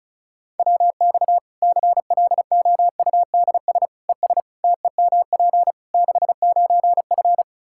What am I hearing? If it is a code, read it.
Morse code, heard as a keyed tone, about 36 wpm, 700 Hz. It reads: WX CLOUDS ES TEMP 69F
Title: K1USN SST QSO Practice 35wpm